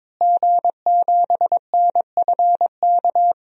Morse code, heard as a keyed tone, about 22 wpm, 700 Hz. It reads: G7NFK
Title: DXCC-100 Call Signs - 5 Characters - Repeated 3x - 22wpm